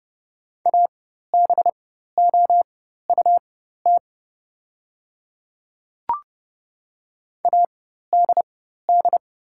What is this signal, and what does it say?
Morse code, heard as a keyed tone, about 31 wpm, spaced as for 14 wpm, 700 Hz. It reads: ABOUT ADD
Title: 4x Character Spacing - Top 200 Words - 30wpm